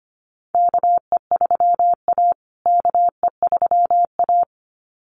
Morse code, heard as a keyed tone, about 25 wpm, 700 Hz. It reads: KE3A KE3A
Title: USA Call Signs Repeated 2x 25wpm